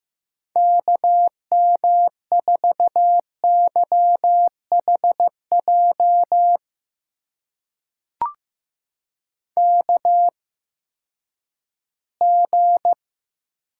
Morse code, heard as a keyed tone, about 15 wpm, spaced as for 10 wpm, 700 Hz. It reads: KM4YHJ KG
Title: USA Call Signs - 8x Character Spacing - 15wpm